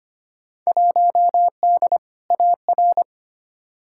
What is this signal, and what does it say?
Morse code, heard as a keyed tone, about 25 wpm, 700 Hz. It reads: 1D AR
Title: Field Day Exchanges - 25wpm